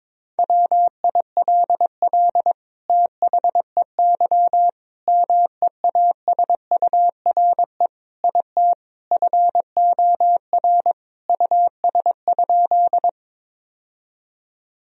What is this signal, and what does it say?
Morse code, heard as a keyed tone, about 21 wpm, 700 Hz. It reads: WILL THEY MEASURE IT FOR US?
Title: Sentences from Top 1000 Words - Review Part 2 - 22wpm